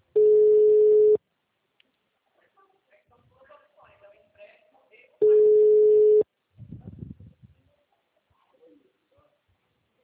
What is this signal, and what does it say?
Ringback tone (Europe, 425 Hz), with speech.